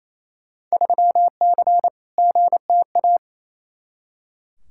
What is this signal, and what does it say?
Morse code, heard as a keyed tone, about 28 wpm, 700 Hz. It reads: 3C GTA
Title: Field Day Exchanges Encore - 28wpm